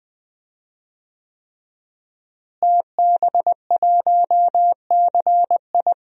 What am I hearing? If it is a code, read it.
Morse code, heard as a keyed tone, about 21 wpm, 700 Hz. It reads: TB1CI
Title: DXCC-100 Call Signs - 5 Characters - Repeated 3x - 20wpm